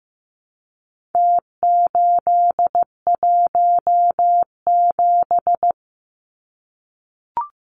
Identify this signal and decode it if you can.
Morse code, heard as a keyed tone, about 15 wpm, 700 Hz. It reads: T817
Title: Numbers - Contest Style - 4-Digit 15wpm